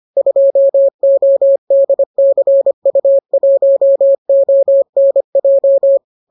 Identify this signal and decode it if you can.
Morse code, heard as a keyed tone, about 25 wpm, 550 Hz. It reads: 2ODCU1ONJ